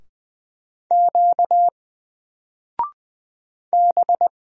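Morse code, heard as a keyed tone, about 20 wpm, 700 Hz. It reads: Q B
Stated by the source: Single letter and number Morse drill at 20 wpm, from N4LCD.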